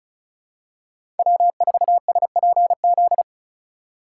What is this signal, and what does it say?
Morse code, heard as a keyed tone, about 34 wpm, 700 Hz. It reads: W4SPZ
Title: USA Call Signs - 5 Characters - 4x Character Spacing - 35wpm